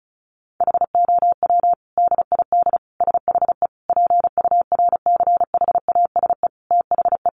Morse code, heard as a keyed tone, about 34 wpm, 700 Hz. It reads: HOW DID SHE PURCHASE THE
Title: Sentences from Top 800 Words 35wpm